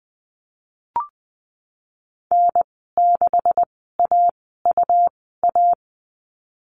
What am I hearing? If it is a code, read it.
Morse code, heard as a keyed tone, about 20 wpm, spaced as for 15 wpm, 700 Hz. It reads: N6AUA
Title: 147 Lesson 24g - Letter y - Call Signs 20wpm